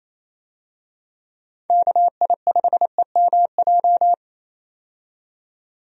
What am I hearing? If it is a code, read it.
Morse code, heard as a keyed tone, about 28 wpm, 700 Hz. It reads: KI5EMJ
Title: USA Call Signs - 6 Characters - 28wpm